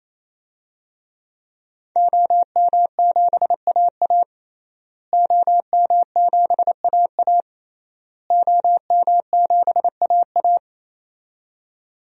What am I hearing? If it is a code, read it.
Morse code, heard as a keyed tone, about 27 wpm, 700 Hz. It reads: OM7AA OM7AA OM7AA
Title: DXCC-100 Call Signs - 5 Characters - Repeated 3x - 28wpm